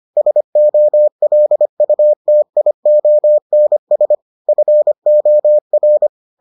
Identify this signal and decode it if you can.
Morse code, heard as a keyed tone, about 25 wpm, 600 Hz. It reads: SOLUTIONS FOR